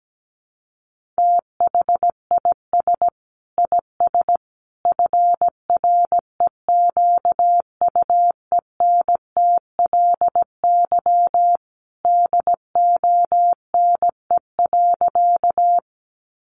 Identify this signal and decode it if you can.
Morse code, heard as a keyed tone, about 17 wpm, 700 Hz. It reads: THIS IS FREQUENTLY DONE.
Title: Sentences from Top 1300 Words 17wpm